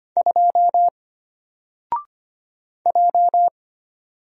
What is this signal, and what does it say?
Morse code, heard as a keyed tone, about 25 wpm, 700 Hz. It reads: 2 J
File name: Single-Letter-Number-25wpm